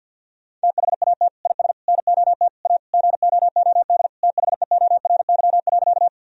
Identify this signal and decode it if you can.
Morse code, heard as a keyed tone, about 50 wpm, 700 Hz. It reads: THAT IS NOT A GOOD THEORY.
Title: Sentences from Top 1000 Words - Review Part 3 - 50wpm